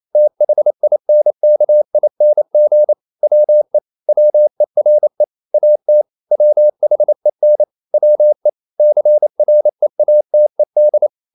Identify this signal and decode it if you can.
Morse code, heard as a keyed tone, about 28 wpm, 600 Hz. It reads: THINKING WE WERE AT WHEN WE CREATED
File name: CW / Practice / EnglishQuotations0228